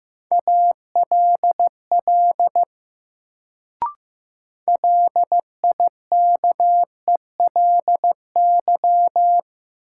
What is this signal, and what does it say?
Morse code, heard as a keyed tone, about 15 wpm, 700 Hz. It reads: ALL LIKELY
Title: Sets of 2 Words - New Word Drill - 15wpm